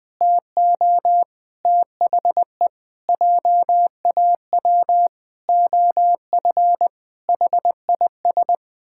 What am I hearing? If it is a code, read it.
Morse code, heard as a keyed tone, about 20 wpm, 700 Hz. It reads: TO THE JAW OF HIS
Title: Pangram Challenge v2 - 20wpm